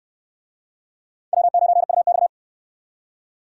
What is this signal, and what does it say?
Morse code, heard as a keyed tone, about 64 wpm, 700 Hz. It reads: W9UX